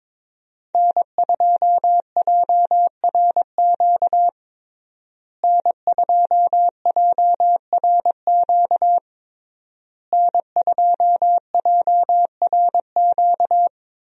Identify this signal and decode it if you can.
Morse code, heard as a keyed tone, about 21 wpm, 700 Hz. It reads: N2JRQ N2JRQ N2JRQ
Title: USA Call Signs - 5 Characters - Repeated 3x - 22wpm